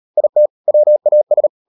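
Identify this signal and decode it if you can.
Morse code, heard as a keyed tone, about 38 wpm, 600 Hz. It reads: IT WAS
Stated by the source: Morse code practice at 38 WPM, sending Ken Follett's novel Never in English, from CW med Gnister.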